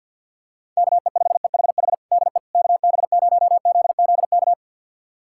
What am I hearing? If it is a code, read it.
Morse code, heard as a keyed tone, about 52 wpm, 700 Hz. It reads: KE5EHH DE KB0ZZK
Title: USA Callsign DE Callsign - 6 Characters - 50wpm